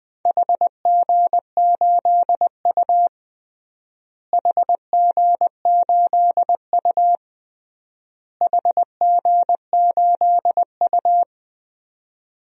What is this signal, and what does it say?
Morse code, heard as a keyed tone, about 20 wpm, 700 Hz. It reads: HG8U HG8U HG8U
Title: DXCC-100 Call Signs - 4 Characters - Repeated 3x - 20wpm